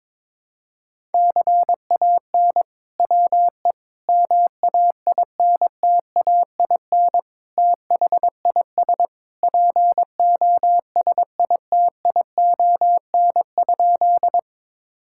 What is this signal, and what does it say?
Morse code, heard as a keyed tone, about 22 wpm, 700 Hz. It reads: CAN WE MAINTAIN THIS POSITION?
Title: Sentences from Top 1000 Words - Review Part 5 - 22wpm